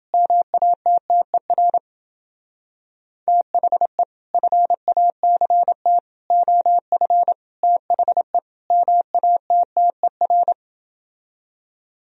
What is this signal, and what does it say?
Morse code, heard as a keyed tone, about 30 wpm, 700 Hz. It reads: MATTER THE FACT OF THE MATTER